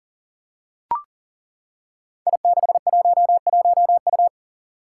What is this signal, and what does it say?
Morse code, heard as a keyed tone, about 40 wpm, 700 Hz. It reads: I611U
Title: Five Challenging Letter-Number Combinations 40wpm